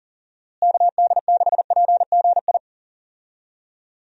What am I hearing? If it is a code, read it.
Morse code, heard as a keyed tone, about 39 wpm, 700 Hz. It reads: KD6PGI
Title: USA Call Signs - 6 Characters - Repeated 3x - 40wpm